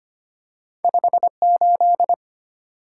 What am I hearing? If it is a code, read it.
Morse code, heard as a keyed tone, about 25 wpm, 700 Hz. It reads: 58
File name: Two Letter-Number Combinations - 25wpm